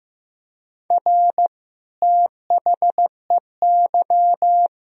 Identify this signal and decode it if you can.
Morse code, heard as a keyed tone, about 15 wpm, 700 Hz. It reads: R THEY
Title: Sentences from Top 400 Words and Abbreviations 15wpm